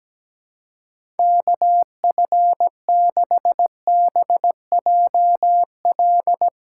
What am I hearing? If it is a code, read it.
Morse code, heard as a keyed tone, about 17 wpm, 700 Hz. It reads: KF6BJL
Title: USA Callsign DE Callsign - 6 Characters - 17wpm